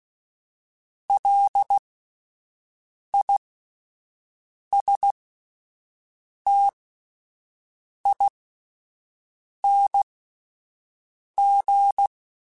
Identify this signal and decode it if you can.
Morse code, heard as a keyed tone, about 16 wpm, spaced as for 5 wpm, 800 Hz. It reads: LISTING
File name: Alphabetical / Alphabetical mp3s / Ls @5